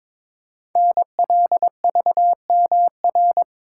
Morse code, heard as a keyed tone, about 22 wpm, 700 Hz. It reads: NLVMR
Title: Code Groups - 5 Letters - 22wpm